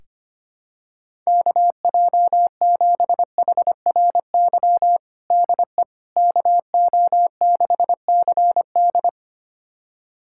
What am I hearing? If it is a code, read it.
Morse code, heard as a keyed tone, about 25 wpm, 700 Hz. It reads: KJ7HRY DE KO6CD
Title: USA Callsign DE Callsign 25wpm